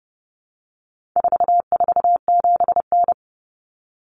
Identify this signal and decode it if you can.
Morse code, heard as a keyed tone, about 30 wpm, 700 Hz. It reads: 447N
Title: Common Cut Numbers - 4-Digit 30wpm